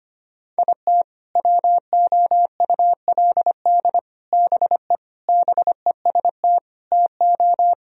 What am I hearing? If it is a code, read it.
Morse code, heard as a keyed tone, about 25 wpm, 700 Hz. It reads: IT WOULD BE BEST TO
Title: Sentences from Top 1000 Words - Review Part 2 - 25wpm